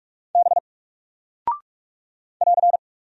Morse code, heard as a keyed tone, about 44 wpm, 700 Hz. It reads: B <AR>
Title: Single Character (Letter:Number:Punctuation:Prosign) 45wpm